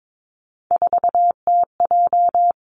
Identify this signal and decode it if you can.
Morse code, heard as a keyed tone, about 22 wpm, 700 Hz. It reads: 4TJ
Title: Three Challenging Letter-Number Combinations 22wpm